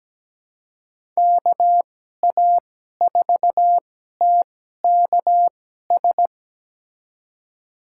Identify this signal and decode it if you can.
Morse code, heard as a keyed tone, about 17 wpm, spaced as for 12 wpm, 700 Hz. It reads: KA4TKS
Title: USA Call Signs - 2x Character Spacing - 17wpm